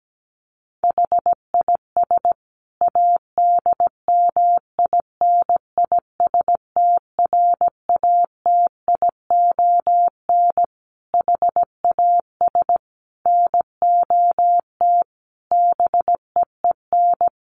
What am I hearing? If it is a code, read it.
Morse code, heard as a keyed tone, about 17 wpm, 700 Hz. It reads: HIS ADMINISTRATION HAS NOT BEEN
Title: Sentences from Top 1300 Words 17wpm